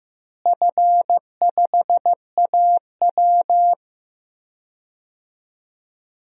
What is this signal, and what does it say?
Morse code, heard as a keyed tone, about 15 wpm, 700 Hz. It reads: F5AW
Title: DXCC-100 Call Signs - 4 Characters - Repeated 3x - 15wpm